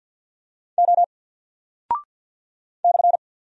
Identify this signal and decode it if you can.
Morse code, heard as a keyed tone, about 48 wpm, 700 Hz. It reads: O /
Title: Single Character (Letter:Number:Punctuation:Prosign) 50wpm